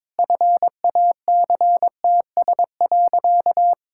Morse code, heard as a keyed tone, about 22 wpm, 700 Hz. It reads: FACTS.
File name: practice / 22wpm / Sentences with 2-Word Phrases - 22wpm